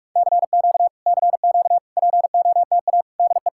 Morse code, heard as a keyed tone, about 44 wpm, 700 Hz. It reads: CQ CQ POTA DE